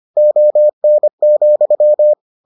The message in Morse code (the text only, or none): ON,